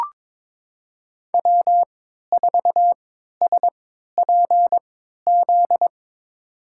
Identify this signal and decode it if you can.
Morse code, heard as a keyed tone, about 22 wpm, spaced as for 13 wpm, 700 Hz. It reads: W4SPZ
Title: USA Call Signs - 3x Character Spacing - 22wpm